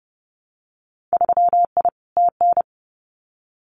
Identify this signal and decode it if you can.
Morse code, heard as a keyed tone, about 30 wpm, 700 Hz. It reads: 3I TN
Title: Winter Field Day Exchanges 30wpm